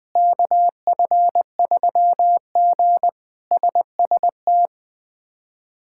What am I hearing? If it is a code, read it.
Morse code, heard as a keyed tone, about 20 wpm, 700 Hz. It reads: KF3G SST
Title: K1USN SST QSO Practice - 20wpm